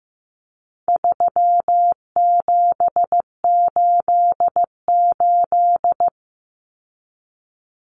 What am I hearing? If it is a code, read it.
Morse code, heard as a keyed tone, about 15 wpm, 700 Hz. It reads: 3788
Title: Common Cut Numbers - 4-Digit 15wpm